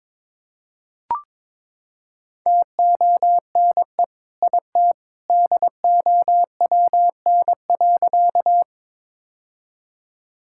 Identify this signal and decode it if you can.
Morse code, heard as a keyed tone, about 21 wpm, 700 Hz. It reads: TONE IT DOWN.
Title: Sentences from Top 1600 Words 22wpm